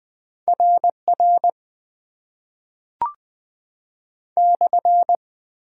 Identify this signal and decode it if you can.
Morse code, heard as a keyed tone, about 20 wpm, 700 Hz. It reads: RR /
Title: Top 90 QSO Elements - 8x Character Spacing - 20wpm